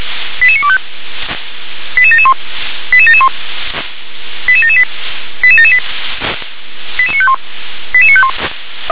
Five-tone selective call (ZVEI), heard as quick runs of five tones, tones of about 70 ms. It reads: 88925 88771 88771 88778 88778 88951 88951
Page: ZVEI